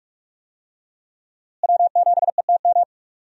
Morse code, heard as a keyed tone, about 46 wpm, 700 Hz. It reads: W7ETM
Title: USA Call Signs - 5 Characters - Repeated 2x - 45wpm